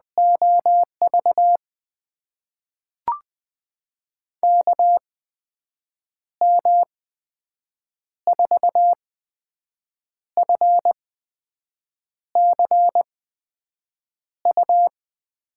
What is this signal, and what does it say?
Morse code, heard as a keyed tone, about 20 wpm, spaced as for 6 wpm, 700 Hz. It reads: OV KM4FCU
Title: USA Call Signs - 6 Characters - 8x Character Spacing - 20wpm